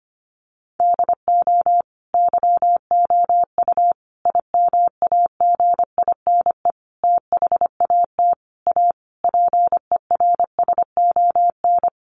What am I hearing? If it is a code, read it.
Morse code, heard as a keyed tone, about 25 wpm, 700 Hz. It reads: DO YOU IMAGINE THAT A PERSON
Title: Sentences from Top 1400 Words 25wpm